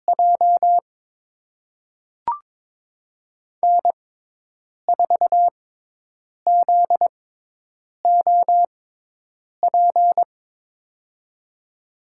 Morse code, heard as a keyed tone, about 22 wpm, spaced as for 8 wpm, 700 Hz. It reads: J N4ZOP
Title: USA Call Signs - 5 Characters - 6x Character Spacing - 22wpm